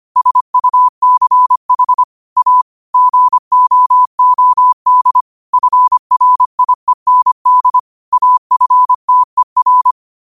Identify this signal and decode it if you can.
Morse code, heard as a keyed tone, about 25 wpm, 1000 Hz. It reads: SUCH A GOOD FRIEND AFTER
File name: qotdfiles / 25wpm